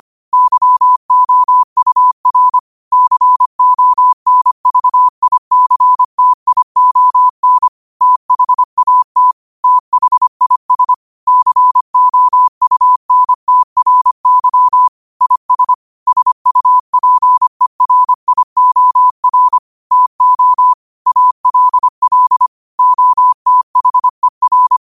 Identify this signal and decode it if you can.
Morse code, heard as a keyed tone, about 25 wpm, 1000 Hz. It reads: YOUR CONVICTION THAT THIS COUNTRY IS SUPERIOR TO ALL OTHER
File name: qotdfiles / 25wpm